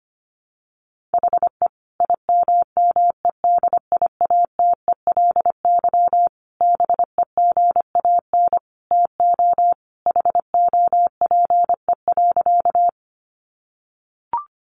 Morse code, heard as a keyed tone, about 25 wpm, 700 Hz. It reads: HE IMMEDIATELY BEGAN TO HOPE.
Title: Sentences from Top 1100 Words 25wpm